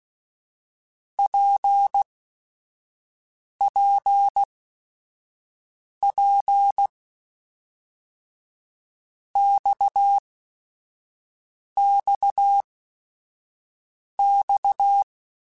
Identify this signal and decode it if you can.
Morse code, heard as a keyed tone, about 16 wpm, spaced as for 5 wpm, 800 Hz. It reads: PPP XXX
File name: Basic Sets / Basic Letter Groups Collective7 / RKPX@5